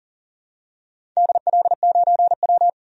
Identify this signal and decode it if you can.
Morse code, heard as a keyed tone, about 40 wpm, 700 Hz. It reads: DL9W
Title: DXCC-100 Call Signs - 4 Characters - 40wpm